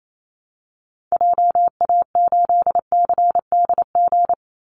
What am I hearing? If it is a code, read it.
Morse code, heard as a keyed tone, about 28 wpm, 700 Hz. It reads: JA8CDG
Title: Call Signs Repeated 3x - 28wpm